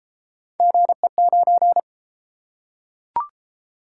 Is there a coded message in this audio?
Yes: Morse code from 0.6 to 1.8 s, heard as a keyed tone, about 31 wpm, 700 Hz. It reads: GE9